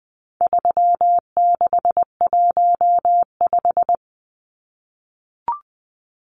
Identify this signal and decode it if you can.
Morse code, heard as a keyed tone, about 20 wpm, 700 Hz. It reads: 3615
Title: Numbers - 4-Digit 20wpm